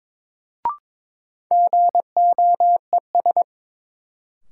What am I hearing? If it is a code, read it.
Morse code, heard as a keyed tone, about 22 wpm, 700 Hz. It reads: GOES
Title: Sentences from Top 1500 Words - New Word Drill Alt - 22wpm